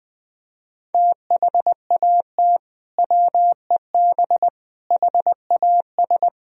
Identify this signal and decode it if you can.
Morse code, heard as a keyed tone, about 20 wpm, 700 Hz. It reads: THAT WEB HAS
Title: Sentences from Top 1000 Words - Review Part 4 - 20wpm